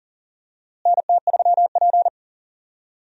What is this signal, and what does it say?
Morse code, heard as a keyed tone, about 40 wpm, 700 Hz. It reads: NT3P
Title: USA Call Signs - 4 Characters - Repeated 3x - 40wpm